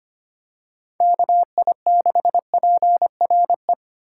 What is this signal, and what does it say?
Morse code, heard as a keyed tone, about 25 wpm, 700 Hz. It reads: KI6PRE
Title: USA Call Signs - 6 Characters - 4x Character Spacing - 25wpm